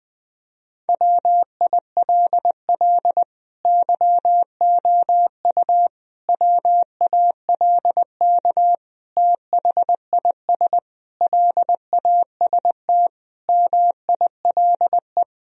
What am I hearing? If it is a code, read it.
Morse code, heard as a keyed tone, about 21 wpm, 700 Hz. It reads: WILL YOU WALK THIS LAST MILE
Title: Sentences from Top 1000 Words - Review Part 1 - 20wpm